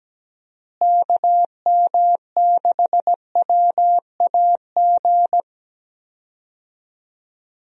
Morse code, heard as a keyed tone, about 17 wpm, 700 Hz. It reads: KM6WAG
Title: USA Call Signs - 6 Characters - Repeated 3x - 17wpm